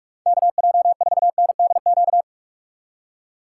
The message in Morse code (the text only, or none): KJ4NDQ